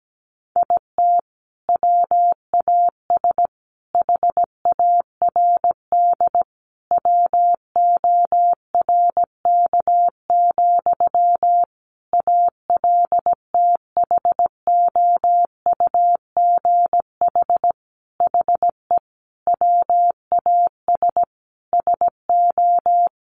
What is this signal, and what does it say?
Morse code, heard as a keyed tone, about 17 wpm, 700 Hz. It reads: IT WAS HARD WORK, ALTHOUGH HE WAS SO
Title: Sentences from Top 600 Words 17wpm